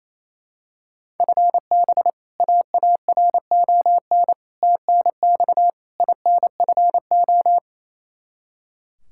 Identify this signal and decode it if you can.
Morse code, heard as a keyed tone, about 28 wpm, 700 Hz. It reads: FB AARON TNX INFO
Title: K1USN SST QSO Practice 28wpm